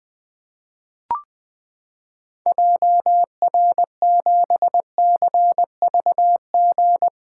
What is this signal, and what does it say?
Morse code, heard as a keyed tone, about 20 wpm, 700 Hz. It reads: JR7CVG
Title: 2023 POTA Activator Call Signs - 20wpm